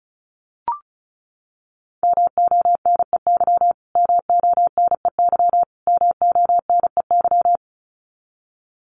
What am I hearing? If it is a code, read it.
Morse code, heard as a keyed tone, about 34 wpm, 700 Hz. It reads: MONEY MONEY MONEY
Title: Top 500 Words Repeated 35wpm